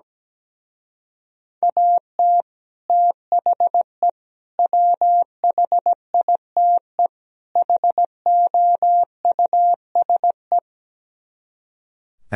Morse code, heard as a keyed tone, about 17 wpm, 700 Hz. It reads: AT THE WHITE HOUSE
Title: Speed-Racing - Sets of 4 Words 17wpm